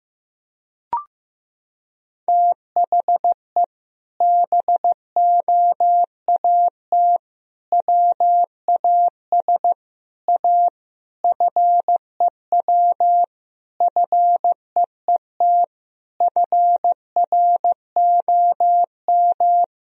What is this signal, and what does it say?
Morse code, heard as a keyed tone, about 15 wpm, 700 Hz. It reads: THE BOAT WAS A FEW FEET FROM
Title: Sentences from Top 1000 Words - Review Part 3 - 15wpm